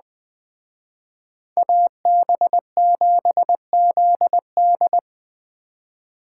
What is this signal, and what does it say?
Morse code, heard as a keyed tone, about 20 wpm, 700 Hz. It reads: AB7ZD